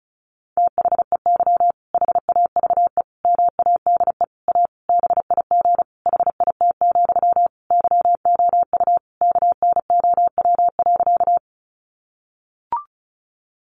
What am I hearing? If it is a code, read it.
Morse code, heard as a keyed tone, about 34 wpm, 700 Hz. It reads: THEY HAVE MADE A BIG HIT, YOU KNOW.
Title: Sentences from Top 1200 Words 35wpm